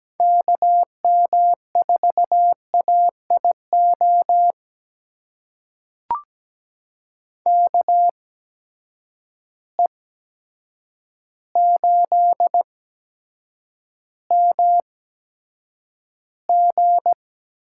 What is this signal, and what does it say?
Morse code, heard as a keyed tone, about 17 wpm, spaced as for 8 wpm, 700 Hz. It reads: KM4AIO KE8MG
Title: USA Call Signs - 6 Characters - 8x Character Spacing - 17wpm